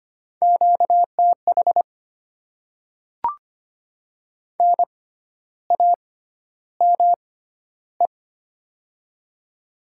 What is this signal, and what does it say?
Morse code, heard as a keyed tone, about 25 wpm, spaced as for 12 wpm, 690 Hz. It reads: QTH NAME
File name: Top 20 QSO Elements - 6x Character Spacing - 25wpm